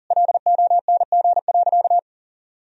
Morse code, heard as a keyed tone, about 37 wpm, 700 Hz. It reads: LONG.
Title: Sentences from Top 1000 Words - Review Part 4 - 40wpm